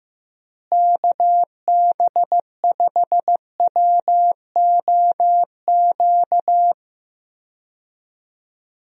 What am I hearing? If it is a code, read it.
Morse code, heard as a keyed tone, about 15 wpm, 700 Hz. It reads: KB5WOQ
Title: USA Call Signs - 6 Characters - Repeated 3x - 15wpm